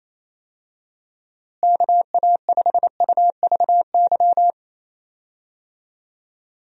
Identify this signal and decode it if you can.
Morse code, heard as a keyed tone, about 28 wpm, 700 Hz. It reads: KA5UVY
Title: USA Call Signs - 6 Characters - Repeated 2x - 28wpm